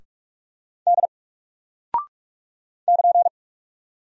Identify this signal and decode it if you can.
Morse code, heard as a keyed tone, about 44 wpm, 700 Hz. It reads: D <KN>
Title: Single Character (Letter:Number:Punctuation:Prosign) 45wpm